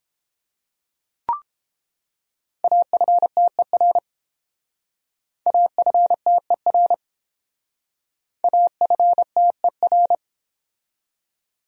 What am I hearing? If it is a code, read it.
Morse code, heard as a keyed tone, about 32 wpm, 700 Hz. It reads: AFTER AFTER AFTER